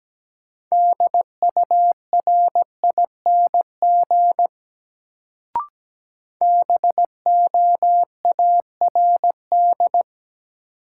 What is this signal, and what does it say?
Morse code, heard as a keyed tone, about 17 wpm, 700 Hz. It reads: DURING BOARD
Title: Sentences from Top 700 Words - New Word Drill - 17wpm